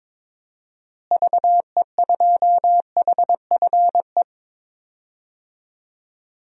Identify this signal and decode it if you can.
Morse code, heard as a keyed tone, about 21 wpm, 700 Hz. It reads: VE2HFE
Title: DXCC-100 Call Signs - 6 Characters - Repeated 2x - 22wpm